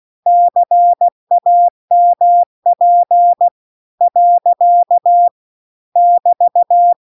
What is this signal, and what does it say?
Morse code, heard as a keyed tone, about 16 wpm, 700 Hz. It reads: CAMP . <BT>
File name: CW-The-Big-Twoheaded-River-2-16WPM